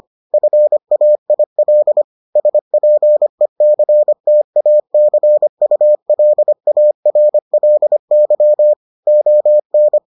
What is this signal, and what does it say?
Morse code, heard as a keyed tone, about 25 wpm, 600 Hz. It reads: FAIL SPECTACULARLY ON